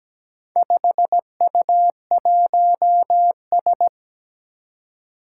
Morse code, heard as a keyed tone, about 17 wpm, 700 Hz. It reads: 5U1S
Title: Four Challenging Letter-Number Combinations 17wpm